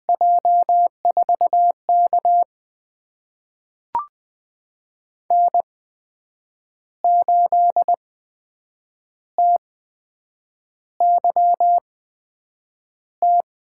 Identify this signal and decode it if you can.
Morse code, heard as a keyed tone, about 20 wpm, spaced as for 7 wpm, 700 Hz. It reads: J4K N8TYT
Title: USA Call Signs - 8x Character Spacing - 20wpm